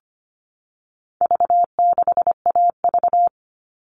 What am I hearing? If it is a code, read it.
Morse code, heard as a keyed tone, about 25 wpm, 700 Hz. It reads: V6AV
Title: All Cut Numbers - 4-Digit 25wpm